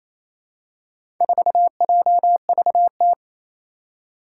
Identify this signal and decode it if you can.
Morse code, heard as a keyed tone, about 28 wpm, 700 Hz. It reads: 4JVT
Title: Four Challenging Letter-Number Combinations 28wpm